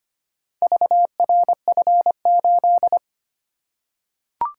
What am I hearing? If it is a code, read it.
Morse code, heard as a keyed tone, about 25 wpm, 700 Hz. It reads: VRF8
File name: practice / 25wpm / Four Letter-Number Combinations - 25wpm